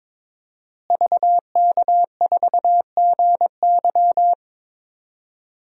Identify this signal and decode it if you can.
Morse code, heard as a keyed tone, about 22 wpm, 700 Hz. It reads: VK4GY
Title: DXCC-100 Call Signs - 5 Characters - Repeated 2x - 22wpm